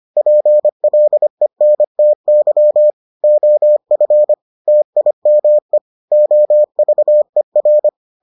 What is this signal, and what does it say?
Morse code, heard as a keyed tone, about 25 wpm, 600 Hz. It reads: PLENTY OF TIME OVER